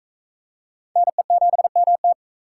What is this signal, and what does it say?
Morse code, heard as a keyed tone, about 42 wpm, 700 Hz. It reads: NE7MT